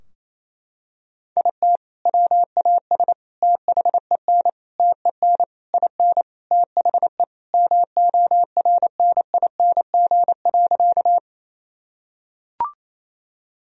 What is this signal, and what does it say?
Morse code, heard as a keyed tone, about 28 wpm, 700 Hz. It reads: IT WAS THEN TEN IN THE MORNING.
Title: Sentences from Top 1000 Words - Review Part 2 - 28wpm